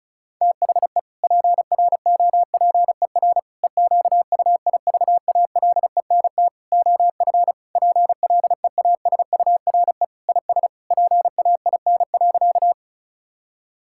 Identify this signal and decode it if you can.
Morse code, heard as a keyed tone, about 34 wpm, 700 Hz. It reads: THE PROPER EQUIVALENT OF PLEASURE IS PAIN.
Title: Sentences from Top 1600 Words 35wpm